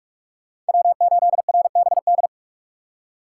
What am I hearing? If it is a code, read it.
Morse code, heard as a keyed tone, about 42 wpm, 690 Hz. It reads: W8RBD
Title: USA Call Signs - 5 Characters - 2x Character Spacing - 45wpm